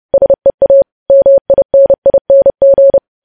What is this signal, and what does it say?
Morse code, heard as a keyed tone, about 30 wpm, 550 Hz. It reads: SEA MINING
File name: podcast / News headlines-1x-30x30